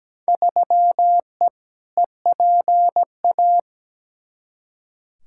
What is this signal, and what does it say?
Morse code, heard as a keyed tone, about 17 wpm, 700 Hz. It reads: 3E EPA
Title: Field Day Exchanges Encore - 17wpm